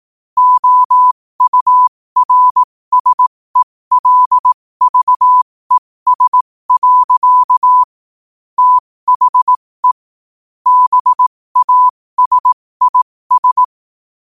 Morse code, heard as a keyed tone, about 18 wpm, spaced as for 15 wpm, 1000 Hz. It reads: OURSELVES. THE BASIS